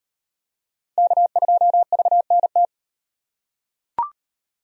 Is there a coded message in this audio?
Yes: Morse code from 1.0 to 2.7 s, heard as a keyed tone, about 35 wpm, 700 Hz. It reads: K2VNT